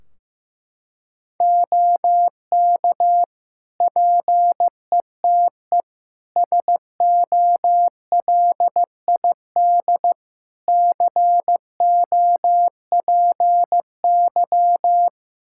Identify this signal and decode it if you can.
Morse code, heard as a keyed tone, about 15 wpm, 700 Hz. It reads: OK PETE SOLID COPY
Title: Practice QSO - Exchange 3 - More Info - 15wpm